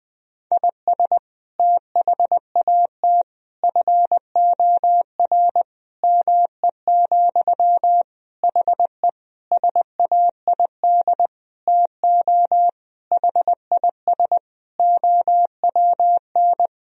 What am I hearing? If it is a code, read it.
Morse code, heard as a keyed tone, about 21 wpm, 700 Hz. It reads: IS THAT FOR ME, HE SAID TO HIS OWN
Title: Sentences from Top 1000 Words - Review Part 1 - 20wpm